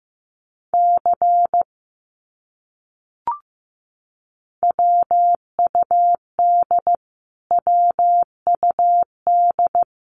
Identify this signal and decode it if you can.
Morse code, heard as a keyed tone, about 15 wpm, 700 Hz. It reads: C WUD WUD
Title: Abbreviations Repeated 3x - 15wpm